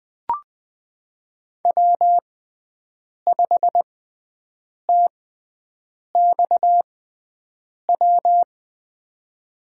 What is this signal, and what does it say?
Morse code, heard as a keyed tone, about 20 wpm, spaced as for 7 wpm, 700 Hz. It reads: W5TXW
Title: USA Call Signs - 5 Characters - 6x Character Spacing - 20wpm